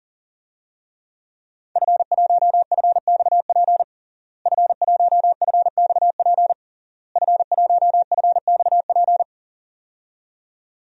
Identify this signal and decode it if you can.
Morse code, heard as a keyed tone, about 40 wpm, 700 Hz. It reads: F1FXP F1FXP F1FXP
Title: DXCC-100 Call Signs - 5 Characters - Repeated 3x - 40wpm